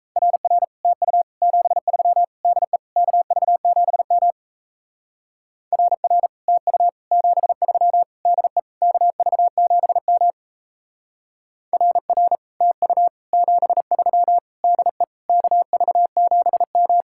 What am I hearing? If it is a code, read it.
Morse code, heard as a keyed tone, about 35 wpm, 700 Hz. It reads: RR TU 73 DE KV7M RR TU 73 DE KV7M RR TU 73 DE KV7M
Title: Speed-Racing Common QSO Phrases - 28wpm